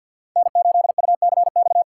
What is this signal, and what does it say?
Morse code, heard as a keyed tone, about 53 wpm, 700 Hz. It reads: N8UCX
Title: USA Call Signs - 5 Characters - 6x Character Spacing - 50wpm